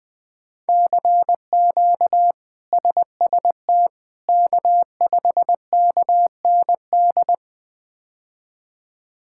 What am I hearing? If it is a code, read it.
Morse code, heard as a keyed tone, about 20 wpm, 700 Hz. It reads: CQ SST K5KND
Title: K1USN SST QSO Practice - 20wpm